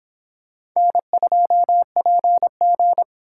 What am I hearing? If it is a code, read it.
Morse code, heard as a keyed tone, about 26 wpm, 700 Hz. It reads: N2PG